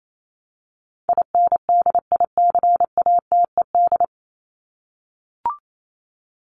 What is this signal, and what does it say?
Morse code, heard as a keyed tone, about 28 wpm, 700 Hz. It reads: INDICATED
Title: ED Suffix - Top 100 Words with Base Words 28wpm